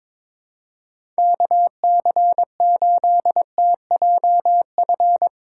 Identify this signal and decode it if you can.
Morse code, heard as a keyed tone, about 22 wpm, 700 Hz. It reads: KC8TJF
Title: USA Call Signs - 2x Character Spacing - 22wpm